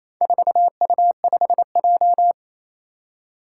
Morse code, heard as a keyed tone, about 28 wpm, 700 Hz. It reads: 4U5J
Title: Four Challenging Letter-Number Combinations 28wpm